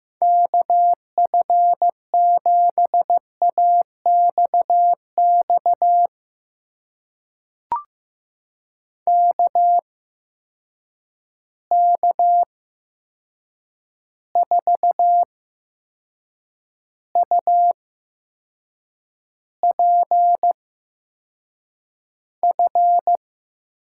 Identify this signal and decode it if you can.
Morse code, heard as a keyed tone, about 15 wpm, spaced as for 6 wpm, 700 Hz. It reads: KF7AXX KK4UPF
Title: USA Call Signs - 6 Characters - 8x Character Spacing - 15wpm